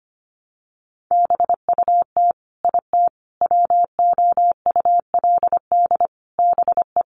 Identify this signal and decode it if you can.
Morse code, heard as a keyed tone, about 25 wpm, 700 Hz. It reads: BUT IT WOULD BE
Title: Sentences from Top 1400 Words 25wpm